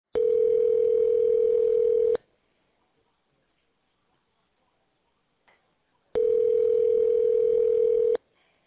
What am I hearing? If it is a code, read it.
Ringback tone (North America), 440 + 480 Hz.